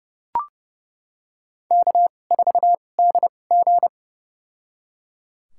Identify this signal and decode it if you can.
Morse code, heard as a keyed tone, about 30 wpm, spaced as for 22 wpm, 700 Hz. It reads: K4DG
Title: USA Call Signs - 4 Characters - 2x Character Spacing - 30wpm